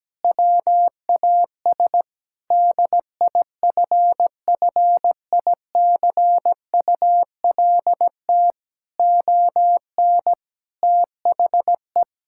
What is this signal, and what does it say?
Morse code, heard as a keyed tone, about 17 wpm, 700 Hz. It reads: WAS DIFFICULT ON THE
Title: Sentences from Top 1000 Words - Review Part 5 - 17wpm